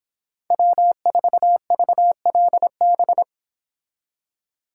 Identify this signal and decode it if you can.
Morse code, heard as a keyed tone, about 26 wpm, 700 Hz. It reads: W4VLB